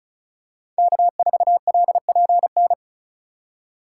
Morse code, heard as a keyed tone, about 36 wpm, 700 Hz. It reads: K4LPN
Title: USA Call Signs - 5 Characters - 2x Character Spacing - 35wpm